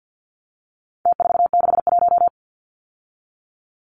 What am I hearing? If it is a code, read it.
Morse code, heard as a keyed tone, about 49 wpm, 700 Hz. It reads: T461
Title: Numbers - Contest Style - 4-Digit 50wpm